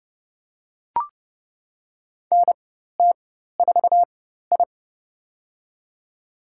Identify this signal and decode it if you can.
Morse code, heard as a keyed tone, about 30 wpm, spaced as for 14 wpm, 700 Hz. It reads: NT4I
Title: USA Call Signs - 4 Characters - 4x Character Spacing - 30wpm